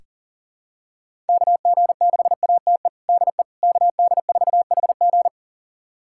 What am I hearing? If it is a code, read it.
Morse code, heard as a keyed tone, about 37 wpm, 700 Hz. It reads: KG6ATE DE KD4HG
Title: USA Callsign DE Callsign 40wpm